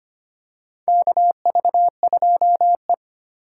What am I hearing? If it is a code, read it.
Morse code, heard as a keyed tone, about 25 wpm, 700 Hz. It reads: KV2E
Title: USA Call Signs - 4 Characters - 8x Character Spacing - 25wpm